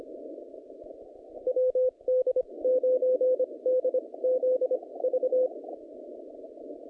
Morse code, heard as a keyed tone, about 25 wpm, spaced as for 20 wpm, 510 Hz. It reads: WD9DZV
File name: contests / naqp-cw-k5tr-op-k5pi- -0100